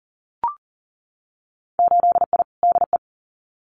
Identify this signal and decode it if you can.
Morse code, heard as a keyed tone, about 40 wpm, 700 Hz. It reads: 8I DE
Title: Winter Field Day Exchanges Encore 40wpm